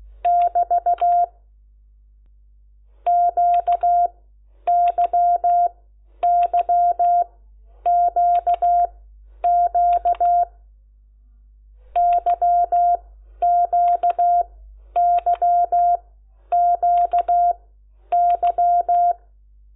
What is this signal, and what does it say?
Morse code, heard as a keyed tone, about 16 wpm, spaced as for 10 wpm, 700 Hz. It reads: <BT> QYYQQ YQYQY
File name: Lecciones